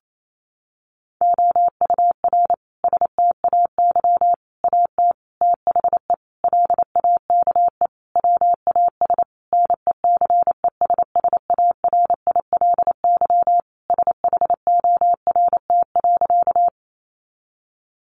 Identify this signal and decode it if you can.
Morse code, heard as a keyed tone, about 28 wpm, 700 Hz. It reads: OUR STAY AT THE LAKE WAS NECESSARILY SHORT.
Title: Sentences from Top 1400 Words 28wpm